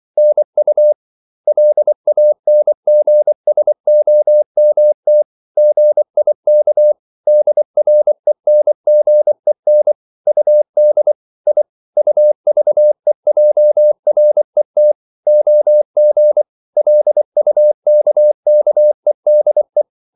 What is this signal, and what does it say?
Morse code, heard as a keyed tone, about 24 wpm, 600 Hz. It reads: NU LANGSOMT GIK DRENGEN UD I UVEJRET OG LUKKEDE
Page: Ferie 24 wpm